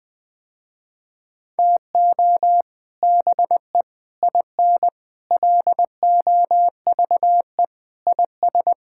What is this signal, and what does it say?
Morse code, heard as a keyed tone, about 21 wpm, 700 Hz. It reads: TO BE IN LOVE IS
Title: Sentences with 3-Word Phrases - Spoken Beginning - 20wpm